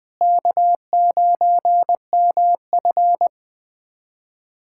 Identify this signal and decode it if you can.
Morse code, heard as a keyed tone, about 21 wpm, 700 Hz. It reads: K9MF
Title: USA Callsign DE Callsign - 4 Characters - 20wpm